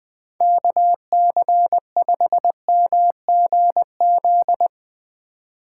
Morse code, heard as a keyed tone, about 22 wpm, 700 Hz. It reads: KC5MGZ
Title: Speed Racing - USA 2x3 Call Signs 20wpm